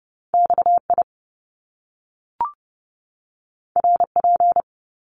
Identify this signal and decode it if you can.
Morse code, heard as a keyed tone, about 30 wpm, 700 Hz. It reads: XI RP
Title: Top Two Letter Combinations in Words 30wpm